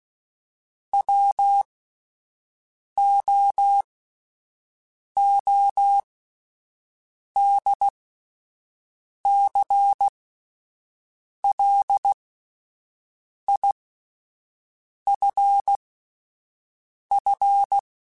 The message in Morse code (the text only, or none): WOODCLIFF